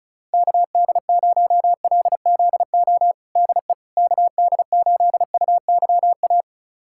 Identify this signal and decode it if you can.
Morse code, heard as a keyed tone, about 36 wpm, 700 Hz. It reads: KD0LZO DE KD8UYA
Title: USA Callsign DE Callsign - 6 Characters - 35wpm